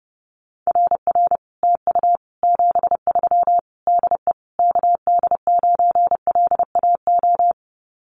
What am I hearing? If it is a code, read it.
Morse code, heard as a keyed tone, about 30 wpm, 700 Hz. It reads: RR TU 73 DE KD9LAO
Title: Common QSO Phrases Encore 30wpm